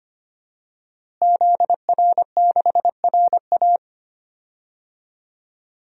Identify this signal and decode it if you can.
Morse code, heard as a keyed tone, about 25 wpm, 700 Hz. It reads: ZR6RA
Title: DXCC-100 Call Signs - 5 Characters - Repeated 3x - 25wpm